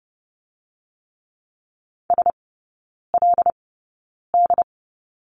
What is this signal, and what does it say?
Morse code, heard as a keyed tone, about 30 wpm, spaced as for 9 wpm, 700 Hz. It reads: SLD
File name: course / 30:10wpm / 087 Lesson 15e - Number 5 - Three Characters 30wpm